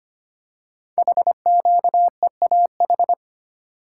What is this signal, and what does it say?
Morse code, heard as a keyed tone, about 25 wpm, 700 Hz. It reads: HQEAH
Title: Code Groups - 5 Letters - 25wpm